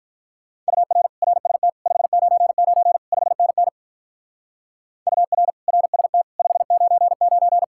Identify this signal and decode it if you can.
Morse code, heard as a keyed tone, about 52 wpm, 700 Hz. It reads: UR RST 599 5NN UR RST 599